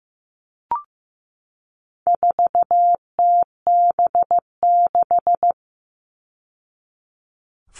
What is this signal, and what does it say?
Morse code, heard as a keyed tone, about 15 wpm, 700 Hz. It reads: 4TB6
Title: All Cut Numbers - 4-Digit 15wpm